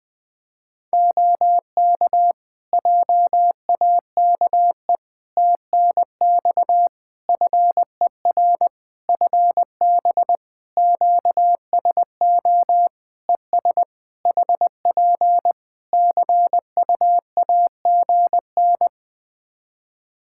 Morse code, heard as a keyed tone, about 20 wpm, 700 Hz. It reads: OK JAKE TNX FER FB QSO ES HP CUAGN
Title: Practice QSO - Exchange 4 - Ending - 20wpm